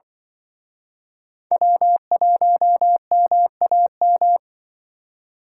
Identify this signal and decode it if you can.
Morse code, heard as a keyed tone, about 23 wpm, 700 Hz. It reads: W1MAM
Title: Speed Racing - USA 1x3 Call Signs 20wpm